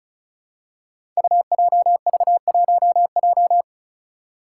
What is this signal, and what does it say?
Morse code, heard as a keyed tone, about 34 wpm, 700 Hz. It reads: UJV1J
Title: Five Challenging Letter-Number Combinations 35wpm